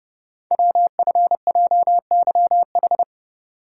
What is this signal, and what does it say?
Morse code, heard as a keyed tone, about 30 wpm, 700 Hz. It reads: WFJYH
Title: Code Groups - 5 Letters - 30wpm